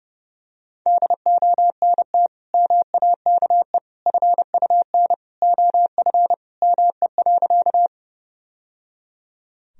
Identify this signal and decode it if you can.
Morse code, heard as a keyed tone, about 30 wpm, 700 Hz. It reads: DONT MAKE FUN OF ME.
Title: Sentences from Top 1000 Words - Review Part 4 - 30wpm